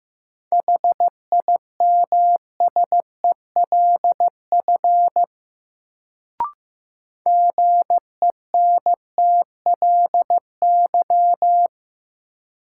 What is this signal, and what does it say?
Morse code, heard as a keyed tone, about 15 wpm, 700 Hz. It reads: HIMSELF GENTLY
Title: Sentences from Top 1500 Words - New Word Drill Alt - 15wpm